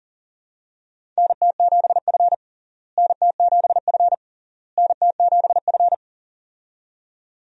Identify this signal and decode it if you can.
Morse code, heard as a keyed tone, about 40 wpm, 700 Hz. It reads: NT7F NT7F NT7F
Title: USA Call Signs - 4 Characters - Repeated 3x - 40wpm